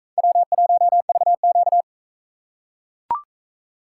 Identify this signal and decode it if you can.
Morse code, heard as a keyed tone, about 42 wpm, 700 Hz. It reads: W1VQ